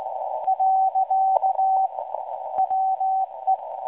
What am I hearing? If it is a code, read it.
Morse code, heard as a keyed tone, about 17 wpm, 760 Hz. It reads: . WE
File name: WebSDR records / SAQ at PI4THT msg 0800-0804 17.2kHz